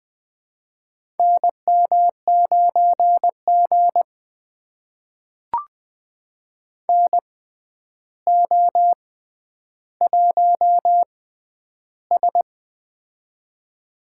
Morse code, heard as a keyed tone, about 20 wpm, spaced as for 10 wpm, 700 Hz. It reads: NM9G NO1S
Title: USA Call Signs - 4 Characters - 6x Character Spacing - 20wpm